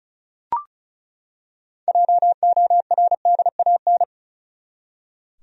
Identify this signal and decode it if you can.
Morse code, heard as a keyed tone, about 34 wpm, 700 Hz. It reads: JORDAN
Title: USA Common Names - 35wpm